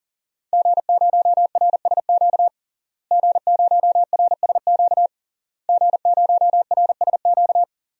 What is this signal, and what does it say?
Morse code, heard as a keyed tone, about 40 wpm, 700 Hz. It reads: G0RSQ G0RSQ G0RSQ
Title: DXCC-100 Call Signs - 5 Characters - Repeated 3x - 40wpm